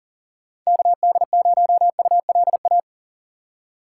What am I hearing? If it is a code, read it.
Morse code, heard as a keyed tone, about 40 wpm, 700 Hz. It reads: KD0ULA
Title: USA Call Signs - 2x Character Spacing - 40wpm